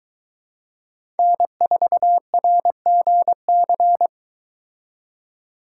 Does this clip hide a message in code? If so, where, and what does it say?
Yes: Morse code from 1.2 to 4.1 s, heard as a keyed tone, about 22 wpm, 700 Hz. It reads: N4RGC